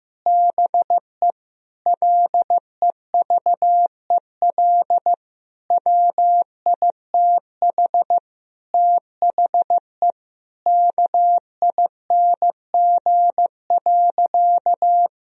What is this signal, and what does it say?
Morse code, heard as a keyed tone, about 16 wpm, 700 Hz. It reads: BE LEVEL WITH THE KING.
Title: Sentences from Top 1000 Words - Review Part 4 - 15wpm